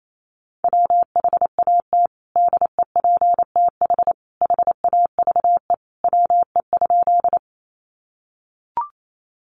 Morse code, heard as a keyed tone, about 28 wpm, 700 Hz. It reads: WHAT DEPTH HAVE WE?
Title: Sentences from Top 1200 Words 28wpm